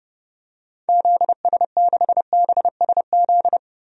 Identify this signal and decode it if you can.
Morse code, heard as a keyed tone, about 32 wpm, 700 Hz. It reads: ZS6BSZ
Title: DXCC-100 Call Signs - 6 Characters - 30wpm